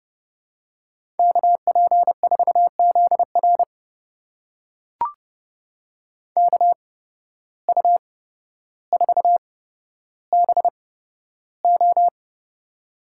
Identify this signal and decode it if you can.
Morse code, heard as a keyed tone, about 30 wpm, spaced as for 13 wpm, 700 Hz. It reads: KP4ZR KU4BO
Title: USA Call Signs - 5 Characters - 8x Character Spacing - 30wpm